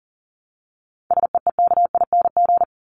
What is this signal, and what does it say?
Morse code, heard as a keyed tone, about 40 wpm, 700 Hz. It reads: SEEKING
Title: ING Suffix - Top 100 Words with Base Words 40wpm